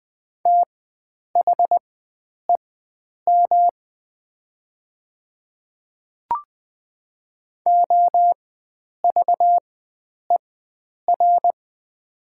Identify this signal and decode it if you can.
Morse code, heard as a keyed tone, about 20 wpm, spaced as for 9 wpm, 700 Hz. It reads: THEM OVER
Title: 4x Character Spacing - Top 100 Words - 20wpm